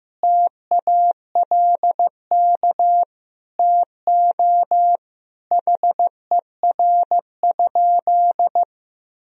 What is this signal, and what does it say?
Morse code, heard as a keyed tone, about 15 wpm, 700 Hz. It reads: TALK TO HER?
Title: Sentences from Top 1000 Words - Review Part 4 - 15wpm